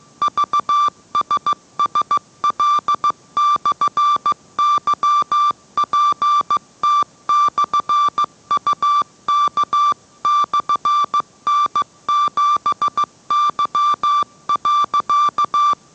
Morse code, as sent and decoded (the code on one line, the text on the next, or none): ...- ... ... .-.. -..-. -.-- .--. - -..-. ..- -.- -..-. -. --... -.-- .-.-.-
VSSL/YPT/UK/N7Y.